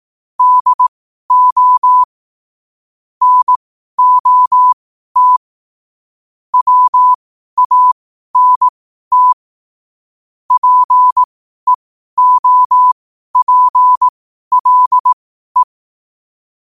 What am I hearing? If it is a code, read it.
Morse code, heard as a keyed tone, about 18 wpm, spaced as for 12 wpm, 1000 Hz. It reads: DO NOT WANT PEOPLE